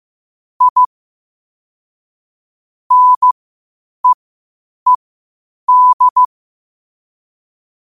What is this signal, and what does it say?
Morse code, heard as a keyed tone, about 15 wpm, spaced as for 8 wpm, 1000 Hz. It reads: I NEED
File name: qotdfiles / 8wpm